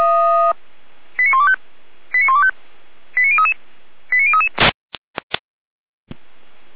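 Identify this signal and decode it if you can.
Five-tone selective call (ZVEI), heard as quick runs of five tones, tones of about 70 ms. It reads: 89125 89125 89033 89033
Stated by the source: ZVEI five-tone selective call 89062 received on an SDR.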